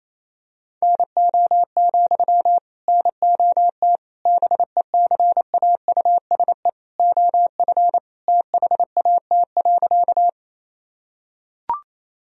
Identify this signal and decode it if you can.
Morse code, heard as a keyed tone, about 28 wpm, 700 Hz. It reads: NO, NOT BECAUSE OF THAT.
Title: Sentences from Top 1000 Words - Review Part 3 - 28wpm